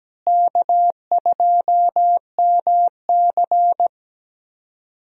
Morse code, heard as a keyed tone, about 17 wpm, 700 Hz. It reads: K2MC
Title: USA Call Signs - 4 Characters - Repeated 2x - 17wpm